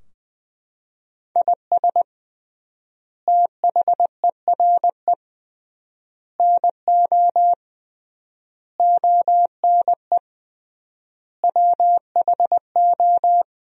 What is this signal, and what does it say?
Morse code, heard as a keyed tone, about 20 wpm, 700 Hz. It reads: IS THERE NO ONE WHO
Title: 3x Word-Spacing - Sentences from Top 200 Words - 20wpm